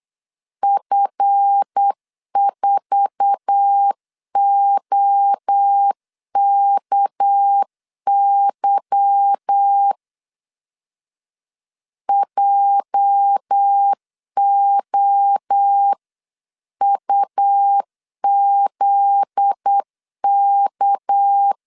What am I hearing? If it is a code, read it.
Morse code, heard as a keyed tone, about 8 wpm, 800 Hz. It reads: F4OKY JOUZK
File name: CW / DET / 1 9 1500C S C 8wpm 5653